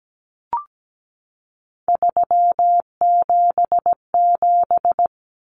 Morse code, heard as a keyed tone, about 17 wpm, 700 Hz. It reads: 377
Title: Common Cut Numbers - 3-Digit 17wpm